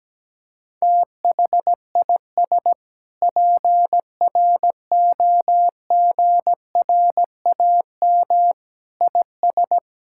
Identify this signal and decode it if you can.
Morse code, heard as a keyed tone, about 17 wpm, 700 Hz. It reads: THIS PROGRAM IS
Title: Sentences from Top 1000 Words - Review Part 5 - 17wpm